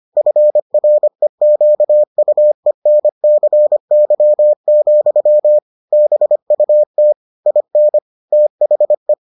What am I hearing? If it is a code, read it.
Morse code, heard as a keyed tone, about 25 wpm, 600 Hz. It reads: FREQUENCY, BUT IN THE